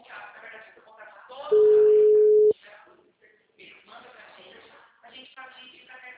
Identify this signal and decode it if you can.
Ringback tone (Europe, 425 Hz), with speech.
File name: out- -2027